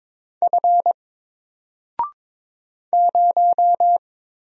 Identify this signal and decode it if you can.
Morse code, heard as a keyed tone, about 22 wpm, 700 Hz. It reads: F 0
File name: Single-Letter-Number-22wpm